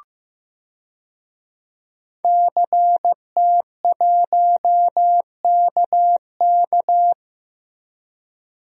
Morse code, heard as a keyed tone, about 15 wpm, 700 Hz. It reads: CT1KK
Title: DXCC-100 Call Signs - 5 Characters - Repeated 3x - 15wpm